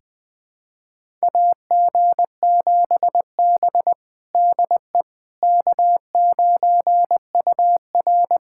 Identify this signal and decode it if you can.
Morse code, heard as a keyed tone, about 21 wpm, 700 Hz. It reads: AG7B DE K9UR
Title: USA Callsign DE Callsign - 4 Characters - 20wpm